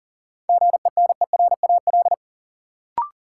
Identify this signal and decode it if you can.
Morse code, heard as a keyed tone, about 40 wpm, 700 Hz. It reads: GENERAL
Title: Sentences from Top 600 Words - New Word Drill - 40wpm